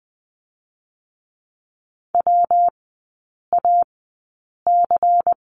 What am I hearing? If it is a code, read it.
Morse code, heard as a keyed tone, about 20 wpm, spaced as for 8 wpm, 700 Hz. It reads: WAC
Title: 121 Lesson 20e - Number 3 - Three Characters 20wpm